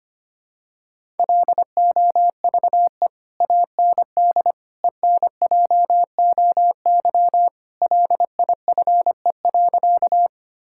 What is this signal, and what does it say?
Morse code, heard as a keyed tone, about 25 wpm, 700 Hz. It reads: LOVE AND ENJOY LIFE.
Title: Sentences from Top 1000 Words - Review Part 5 - 25wpm